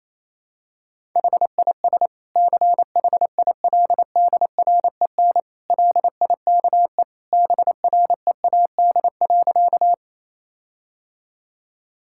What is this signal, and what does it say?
Morse code, heard as a keyed tone, about 28 wpm, 700 Hz. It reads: HIS CHILDREN LIKE BREAD.
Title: Sentences from Top 1600 Words 28wpm